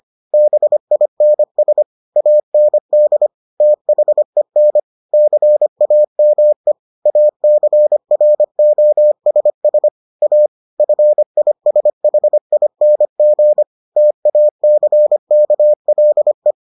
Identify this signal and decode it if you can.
Morse code, heard as a keyed tone, about 25 wpm, 600 Hz. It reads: BINS AND THEN CAME ACROSS A FISHING TACKLE